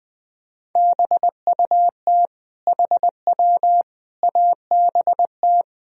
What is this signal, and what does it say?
Morse code, heard as a keyed tone, about 21 wpm, 700 Hz. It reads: BUT HW ABT
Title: Sentences from Top 100 Words and Abbreviations 20wpm